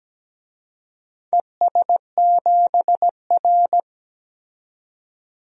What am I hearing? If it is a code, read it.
Morse code, heard as a keyed tone, about 17 wpm, 700 Hz. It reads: ES7R
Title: DXCC-100 Call Signs - 4 Characters - 17wpm